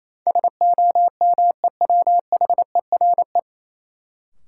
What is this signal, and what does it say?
Morse code, heard as a keyed tone, about 28 wpm, 700 Hz. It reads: SOMEWHERE
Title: Sentences from Top 1400 Words - New Word Drill Alt - 28wpm